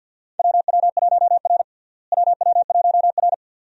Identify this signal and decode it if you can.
Morse code, heard as a keyed tone, about 51 wpm, 700 Hz. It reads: WW1R WW1R
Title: USA Call Signs - 4 Characters - Repeated 2x - 50wpm